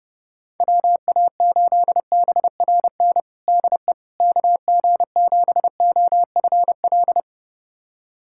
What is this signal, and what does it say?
Morse code, heard as a keyed tone, about 30 wpm, 700 Hz. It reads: WA8BRN DE KG7OFL
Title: USA Callsign DE Callsign - 6 Characters - 30wpm